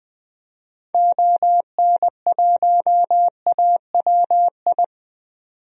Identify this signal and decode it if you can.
Morse code, heard as a keyed tone, about 20 wpm, 700 Hz. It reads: ON1AWI
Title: DXCC-100 Call Signs - 6 Characters - 20wpm